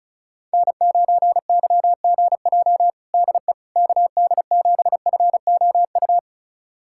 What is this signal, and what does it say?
Morse code, heard as a keyed tone, about 36 wpm, 690 Hz. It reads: N9YGJ DE KD7FOU
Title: USA Callsign DE Callsign 35wpm